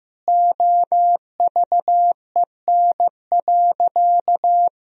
Morse code, heard as a keyed tone, about 15 wpm, 700 Hz. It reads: OVEN.
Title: Sentences from Top 1000 Words - Review Part 4 - 15wpm